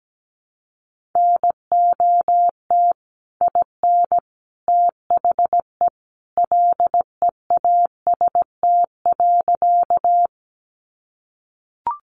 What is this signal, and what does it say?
Morse code, heard as a keyed tone, about 17 wpm, 700 Hz. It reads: NOT IN THE LEAST.
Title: Sentences from Top 700 Words 17wpm